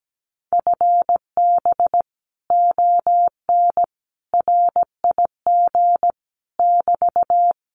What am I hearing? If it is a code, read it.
Morse code, heard as a keyed tone, about 17 wpm, 700 Hz. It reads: FB ON RIG <BT>
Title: Common QSO Phrases Encore 17wpm